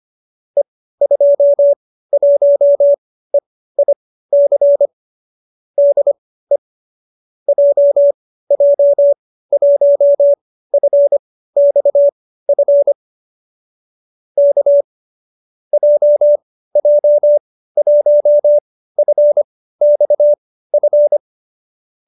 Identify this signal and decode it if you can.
Morse code, heard as a keyed tone, about 25 wpm, spaced as for 15 wpm, 570 Hz. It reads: E21EIC DE JJ1FXF K JJ1FXF